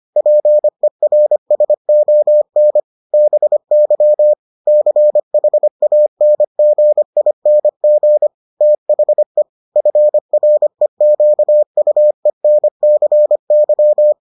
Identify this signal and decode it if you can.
Morse code, heard as a keyed tone, about 25 wpm, 600 Hz. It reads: PERSON BY CHANGING THE FREQUENCY